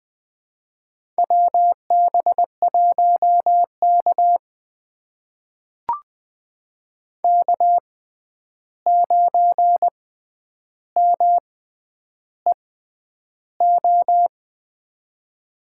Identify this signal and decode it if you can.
Morse code, heard as a keyed tone, about 20 wpm, spaced as for 10 wpm, 700 Hz. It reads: WB1K K9MEO
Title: USA Call Signs - 6x Character Spacing - 20wpm